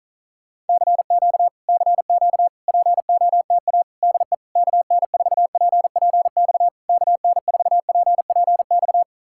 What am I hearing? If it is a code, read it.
Morse code, heard as a keyed tone, about 48 wpm, 700 Hz. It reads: CQ CQ POTA DE KN4PPX KN4PPX